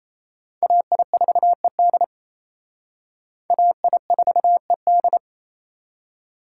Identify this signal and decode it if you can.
Morse code, heard as a keyed tone, about 35 wpm, 700 Hz. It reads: AI4ED AI4ED
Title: Speed Racing - USA 2x2 Call Signs 28wpm